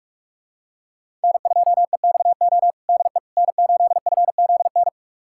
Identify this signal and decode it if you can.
Morse code, heard as a keyed tone, about 42 wpm, 700 Hz. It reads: N2EXO DE N8FZN
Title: USA Callsign DE Callsign - 5 Characters - 45wpm